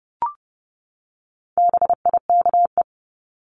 Morse code, heard as a keyed tone, about 30 wpm, 700 Hz. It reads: BIKE
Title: Top 1751-2000 Words 30wpm